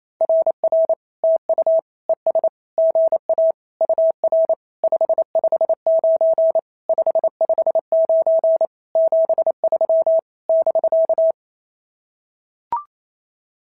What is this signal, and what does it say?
Morse code, heard as a keyed tone, about 27 wpm, 650 Hz. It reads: RR TU ES GA UR 559 559 73 <BK>
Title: SOTA QSO Practice 28wpm